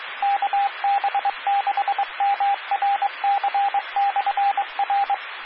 Morse code, heard as a keyed tone, about 23 wpm, 800 Hz, with radio noise. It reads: KB6MRC/R
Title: Claremont, CA Police, Amateur Radio